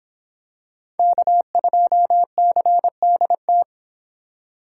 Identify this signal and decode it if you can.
Morse code, heard as a keyed tone, about 23 wpm, 700 Hz. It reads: K2CDT